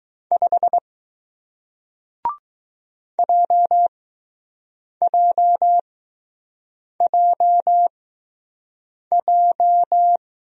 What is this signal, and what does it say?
Morse code, heard as a keyed tone, about 20 wpm, spaced as for 6 wpm, 700 Hz. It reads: 5 JJJJ